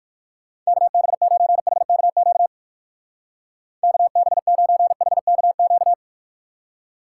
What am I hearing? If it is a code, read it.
Morse code, heard as a keyed tone, about 57 wpm, 700 Hz. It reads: KB9HKQ KB9HKQ